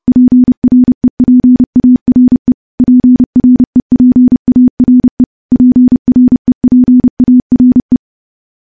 Morse code, heard as a keyed tone, about 30 wpm, 250 Hz. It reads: PREPARE PREPARE PREPARE
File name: morse-training-30WPM